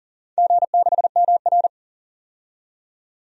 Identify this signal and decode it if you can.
Morse code, heard as a keyed tone, about 42 wpm, 700 Hz. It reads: G6MR
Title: DXCC-100 Call Signs - 4 Characters - 40wpm